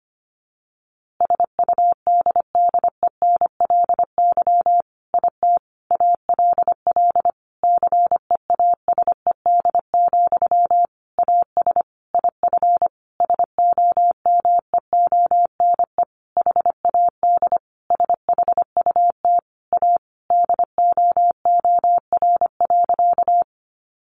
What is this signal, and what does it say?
Morse code, heard as a keyed tone, about 25 wpm, 700 Hz. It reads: SUDDENLY IT ALL CEASED, AS IF SOMEONE HAD SHUT A DOOR.
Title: A Sound Of Thunder 25wpm